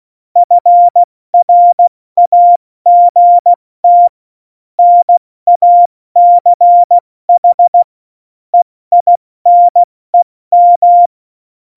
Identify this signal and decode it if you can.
Morse code, heard as a keyed tone, about 16 wpm, 700 Hz. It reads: FRAGT NACH EINEM
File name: Morsekurs / Witze / Witz 15. Januar 2019, 70 F80